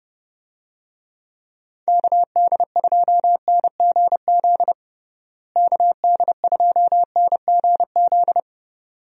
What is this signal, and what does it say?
Morse code, heard as a keyed tone, about 30 wpm, 700 Hz. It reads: KD2NGZ KD2NGZ
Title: USA Call Signs - 6 Characters - Repeated 2x - 30wpm